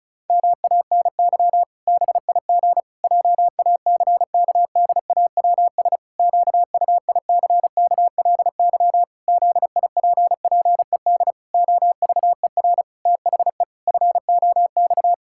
Morse code, heard as a keyed tone, about 36 wpm, 700 Hz. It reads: MANY BIG JACKDAWS QUICKLY ZIPPED OVER THE FOX
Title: Pangram Challenge v2 - 35wpm